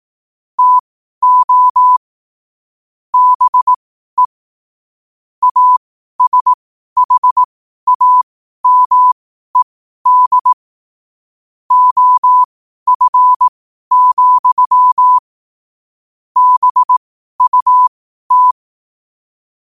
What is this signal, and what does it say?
Morse code, heard as a keyed tone, about 18 wpm, spaced as for 12 wpm, 1000 Hz. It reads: TO BE ASHAMED OF, BUT
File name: qotdfiles / 12wpm